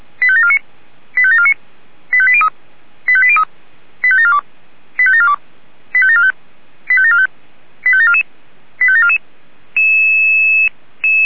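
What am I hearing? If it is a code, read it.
Five-tone selective call (ZVEI), heard as quick runs of five tones, tones of about 70 ms. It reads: 85749 85749 85803 85803 85742 85742 85745 85745 85744 85744
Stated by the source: Five-tone ZVEI selective call, SDR-received, address 85730.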